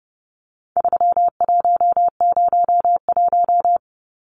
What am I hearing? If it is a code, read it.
Morse code, heard as a keyed tone, about 30 wpm, 700 Hz. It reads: 3101
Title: Numbers - 4-Digit 30wpm